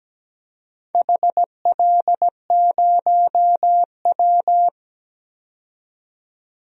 Morse code, heard as a keyed tone, about 17 wpm, 700 Hz. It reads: HL0W
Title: DXCC-100 Call Signs - 4 Characters - 17wpm